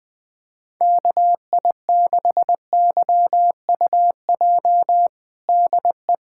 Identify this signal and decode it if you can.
Morse code, heard as a keyed tone, about 20 wpm, 700 Hz. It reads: KI6YUJ DE
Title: USA Callsign DE Callsign 20wpm